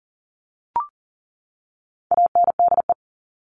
Morse code, heard as a keyed tone, about 40 wpm, 700 Hz. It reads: ANDE
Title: All Cut Numbers - 4-Digit 40wpm